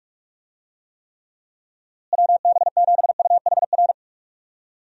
Morse code, heard as a keyed tone, about 45 wpm, 700 Hz. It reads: WB7UHR
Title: USA Call Signs - 6 Characters - Repeated 3x - 45wpm